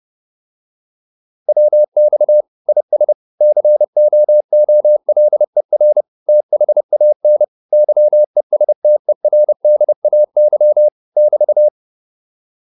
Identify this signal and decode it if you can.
Morse code, heard as a keyed tone, about 30 wpm, 600 Hz. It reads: WX IS COOLER THAN YESTERDAY <BT>